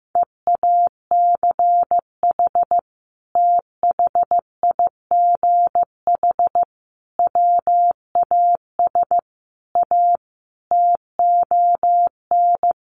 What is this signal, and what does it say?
Morse code, heard as a keyed tone, about 15 wpm, 700 Hz. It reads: EACH THIGH WAS A TON
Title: A Sound Of Thunder 15wpm